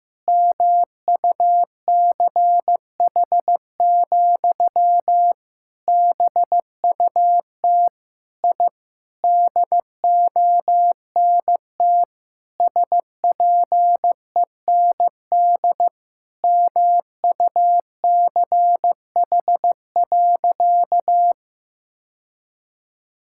Morse code, heard as a keyed tone, about 15 wpm, 700 Hz. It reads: MUCH, BUT I DONT SPEND MUCH.
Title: Sentences from Top 1000 Words - Review Part 5 - 15wpm